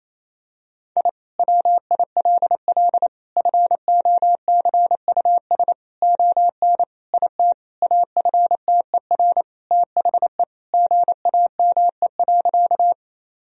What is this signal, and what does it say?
Morse code, heard as a keyed tone, about 28 wpm, 700 Hz. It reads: I WILL FOCUS ON IT AFTER THE GAME.
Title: Sentences from Top 1000 Words - Review Part 4 - 28wpm